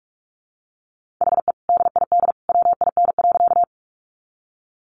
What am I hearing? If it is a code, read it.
Morse code, heard as a keyed tone, about 44 wpm, 700 Hz. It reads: HE DID WIN.
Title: Sentences from Top 1500 Words 45wpm